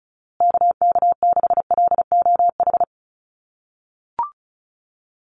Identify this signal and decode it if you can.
Morse code, heard as a keyed tone, about 34 wpm, 700 Hz. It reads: KK6LOH
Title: USA Call Signs Repeated 2x 35wpm